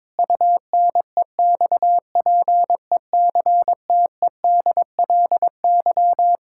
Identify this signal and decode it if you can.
Morse code, heard as a keyed tone, about 21 wpm, 700 Hz. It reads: UNEXPECTEDLY
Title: Pangram Challenge v2 - 22wpm